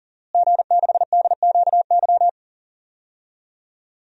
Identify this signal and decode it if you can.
Morse code, heard as a keyed tone, about 40 wpm, 700 Hz. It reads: G6DQY
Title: DXCC-100 Call Signs - 5 Characters - Repeated 2x - 40wpm